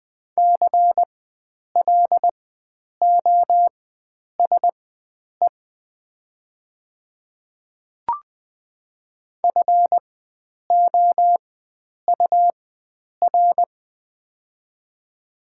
Morse code, heard as a keyed tone, about 20 wpm, spaced as for 9 wpm, 700 Hz. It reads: CLOSE FOUR
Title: 4x Character Spacing - Top 300 Words - 20wpm